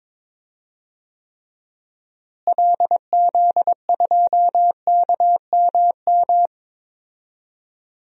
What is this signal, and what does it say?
Morse code, heard as a keyed tone, about 22 wpm, 700 Hz. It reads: LZ2KMM
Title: DXCC-100 Call Signs - 6 Characters - Repeated 2x - 22wpm